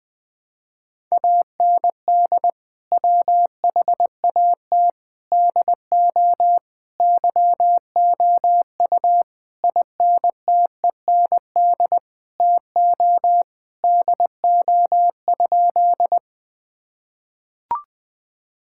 Morse code, heard as a keyed tone, about 20 wpm, 700 Hz. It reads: AND WHAT DO YOU INTEND TO DO?
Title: Sentences from Top 1600 Words 20wpm